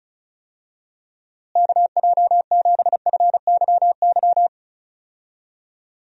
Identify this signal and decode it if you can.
Morse code, heard as a keyed tone, about 36 wpm, 700 Hz. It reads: KJ7FYY
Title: USA Call Signs - 6 Characters - Repeated 3x - 35wpm